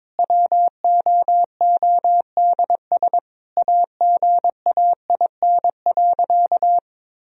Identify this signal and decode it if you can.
Morse code, heard as a keyed tone, about 22 wpm, 700 Hz. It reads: WOODS AGAIN.
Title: Sentences from Top 1000 Words - Review Part 2 - 22wpm